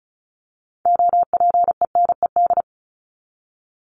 Morse code, heard as a keyed tone, about 34 wpm, 700 Hz. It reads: OPENED
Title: ED Suffix - Top 100 Words with Base Words 35wpm